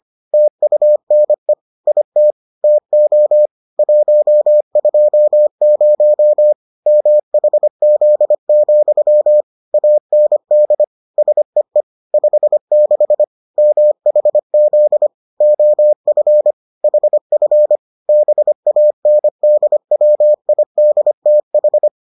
Morse code, heard as a keyed tone, about 25 wpm, 600 Hz. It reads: TUNE IT TO 120 MHZ, AND SEE 56 MHZ OF HF BANDWIDTH